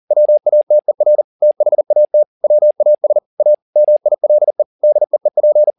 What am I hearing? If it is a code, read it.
Morse code, heard as a keyed tone, about 40 wpm, 600 Hz. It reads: WATER THAT WAS A MILE DEEP